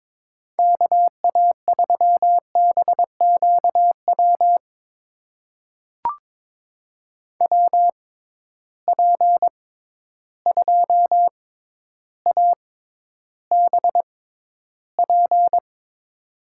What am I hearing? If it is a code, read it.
Morse code, heard as a keyed tone, about 22 wpm, spaced as for 11 wpm, 700 Hz. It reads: KA3BQW WP2ABP
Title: USA Call Signs - 6 Characters - 6x Character Spacing - 22wpm